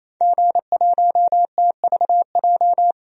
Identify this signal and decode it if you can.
Morse code, heard as a keyed tone, about 27 wpm, 700 Hz. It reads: G1TVJ
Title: DXCC-100 Call Signs - 5 Characters - 28wpm